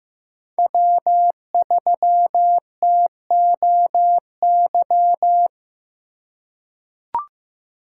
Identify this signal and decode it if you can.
Morse code, heard as a keyed tone, about 15 wpm, 700 Hz. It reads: W3TOY
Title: USA Call Signs - 2x Character Spacing - 15wpm